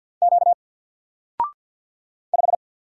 Morse code, heard as a keyed tone, about 49 wpm, 700 Hz. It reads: Q 5
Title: Single Character (Letter:Number:Punctuation:Prosign) Alt 50wpm